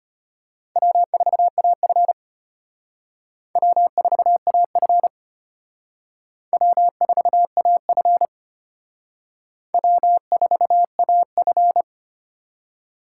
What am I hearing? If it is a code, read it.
Morse code, heard as a keyed tone, about 31 wpm, 700 Hz. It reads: W4AF W4AF W4AF W4AF
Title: Speed Racing - USA 1x2 Call Signs 25wpm